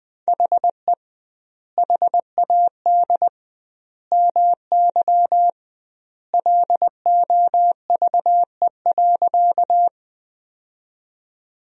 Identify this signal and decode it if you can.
Morse code, heard as a keyed tone, about 20 wpm, 700 Hz. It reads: HE HAD MY LOVE.
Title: 2x Word-Spacing - Sentences from Top 400 Words - 20wpm